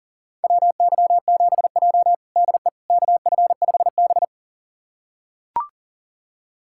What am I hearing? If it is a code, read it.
Morse code, heard as a keyed tone, about 40 wpm, 700 Hz. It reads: WY7J DE KF5B
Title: USA Callsign DE Callsign - 4 Characters - 40wpm